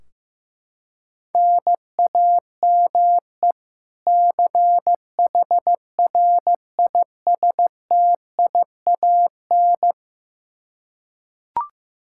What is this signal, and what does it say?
Morse code, heard as a keyed tone, about 15 wpm, 700 Hz. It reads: NAME CHRISTIAN
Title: Practice QSO - Exchange 1 - Beginning - 15wpm